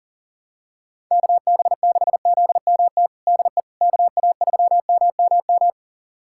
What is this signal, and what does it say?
Morse code, heard as a keyed tone, about 40 wpm, 700 Hz. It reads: KB6ZMT DE KA3MMM
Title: USA Callsign DE Callsign 40wpm